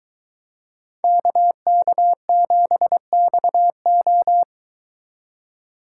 Morse code, heard as a keyed tone, about 24 wpm, 700 Hz. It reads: KK7XO